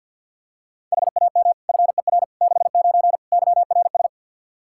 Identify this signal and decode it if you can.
Morse code, heard as a keyed tone, about 53 wpm, 700 Hz. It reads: HAM FER 69 YRS
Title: K1USN SST QSO Practice 50wpm